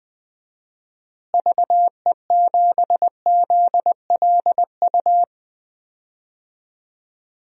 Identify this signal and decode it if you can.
Morse code, heard as a keyed tone, about 20 wpm, 700 Hz. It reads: VE7ZLU
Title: DXCC-100 Call Signs - 6 Characters - Repeated 3x - 20wpm